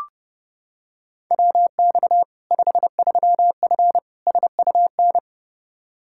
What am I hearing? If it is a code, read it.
Morse code, heard as a keyed tone, about 30 wpm, 700 Hz. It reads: WX 53F SUN
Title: Practice QSO - Exchange 2 - Info - 30wpm